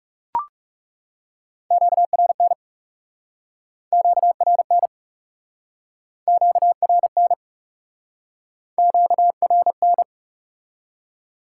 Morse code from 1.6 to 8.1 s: QRN QRN QRN

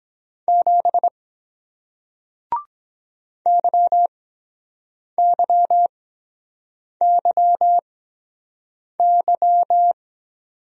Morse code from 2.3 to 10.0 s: YYYY

